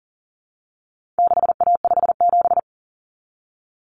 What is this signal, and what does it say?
Morse code, heard as a keyed tone, about 40 wpm, 700 Hz. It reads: 6A57
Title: Common Cut Numbers - 4-Digit 40wpm